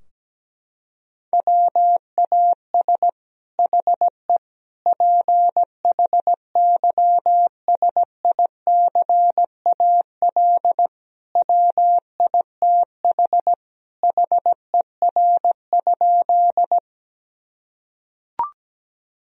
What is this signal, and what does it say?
Morse code, heard as a keyed tone, about 17 wpm, 700 Hz. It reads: WAS HE PHYSICAL WITH HER?
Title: Sentences from Top 1000 Words - Review Part 4 - 17wpm